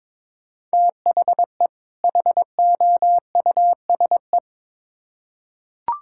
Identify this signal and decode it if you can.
Morse code, heard as a keyed tone, about 22 wpm, 700 Hz. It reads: THE HOUSE
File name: Build A Phrase - 2-Word Phrases - 22wpm